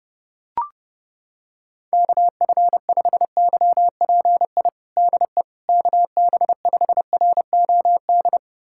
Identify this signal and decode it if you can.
Morse code, heard as a keyed tone, about 30 wpm, 700 Hz. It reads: KF5YPI DE KB5ROD
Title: USA Callsign DE Callsign 30wpm